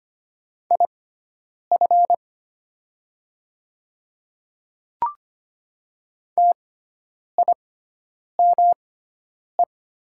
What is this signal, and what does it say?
Morse code, heard as a keyed tone, about 25 wpm, spaced as for 9 wpm, 700 Hz. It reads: IF TIME
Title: 6x Character Spacing - Top 100 Words - 25wpm